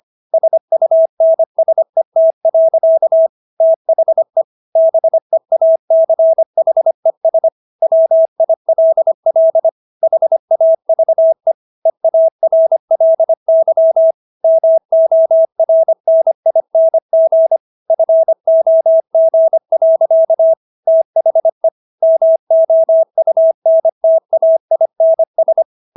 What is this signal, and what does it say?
Morse code, heard as a keyed tone, about 25 wpm, 650 Hz. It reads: SUNSET. THE BEACHES WILL HAVE EARLY MORNING FOG. THE MOUNTAINS